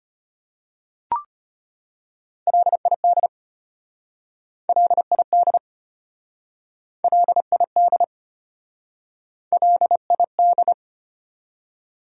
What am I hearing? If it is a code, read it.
Morse code, heard as a keyed tone, about 32 wpm, 700 Hz. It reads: LID LID LID LID